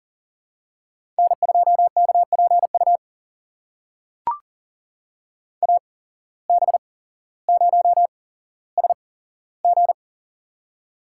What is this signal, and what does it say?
Morse code, heard as a keyed tone, about 40 wpm, spaced as for 17 wpm, 700 Hz. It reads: N2KPU AB0SG
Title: USA Call Signs - 5 Characters - 8x Character Spacing - 40wpm